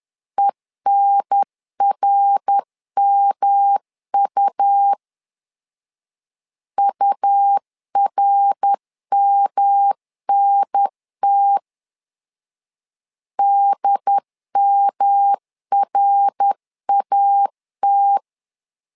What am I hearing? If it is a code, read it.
Morse code, heard as a keyed tone, about 10 wpm, 800 Hz. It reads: ENRMU URMNT DMRAT
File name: LEZIONI / 3 MWRO / 3 MWR 10wpm 1952